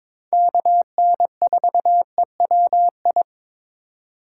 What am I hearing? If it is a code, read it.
Morse code, heard as a keyed tone, about 22 wpm, 700 Hz. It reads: KN4EWI
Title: K1USN SST QSO Practice - 22wpm